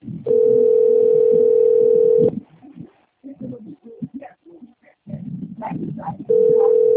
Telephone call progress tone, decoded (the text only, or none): ringback tone